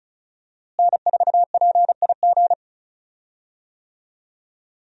Morse code, heard as a keyed tone, about 36 wpm, 700 Hz. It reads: N4PIG
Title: USA Call Signs - 5 Characters - Repeated 2x - 35wpm